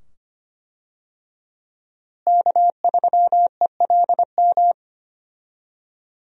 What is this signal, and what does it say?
Morse code, heard as a keyed tone, about 25 wpm, 700 Hz. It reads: K3ELM
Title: USA Call Signs - 5 Characters - Repeated 2x - 25wpm